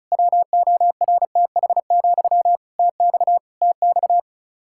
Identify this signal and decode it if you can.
Morse code, heard as a keyed tone, about 36 wpm, 700 Hz. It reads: WORTH, TX TX
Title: Practice QSO - Exchange 1 - Beginning - 35wpm